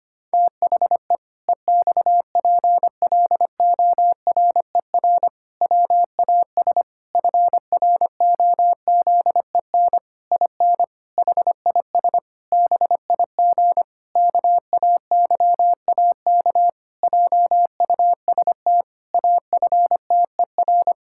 Morse code, heard as a keyed tone, about 25 wpm, 700 Hz. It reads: THE EXPLORER WAS FROZEN IN HIS BIG KAYAK JUST AFTER
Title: Pangram Challenge v2 - 25wpm